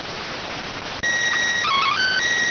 SSTV (slow-scan television), PD 290.